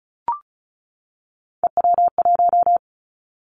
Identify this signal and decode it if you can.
Morse code, heard as a keyed tone, about 34 wpm, 700 Hz. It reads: EW1
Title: Callsign Prefixes for Top 100 DXCC - 35wpm